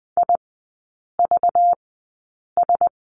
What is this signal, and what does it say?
Morse code, heard as a keyed tone, about 20 wpm, spaced as for 8 wpm, 690 Hz. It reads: IVS
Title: 191 Lesson 31e - Letter v - Three Characters 20wpm